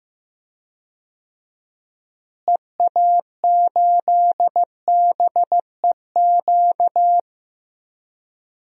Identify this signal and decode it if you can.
Morse code, heard as a keyed tone, about 15 wpm, 700 Hz. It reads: EA8BEQ
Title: DXCC-100 Call Signs - 6 Characters - Repeated 2x - 15wpm